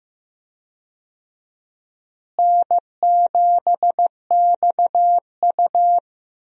Morse code, heard as a keyed tone, about 15 wpm, 700 Hz. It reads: N7XU
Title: USA Call Signs - 4 Characters - Repeated 3x - 15wpm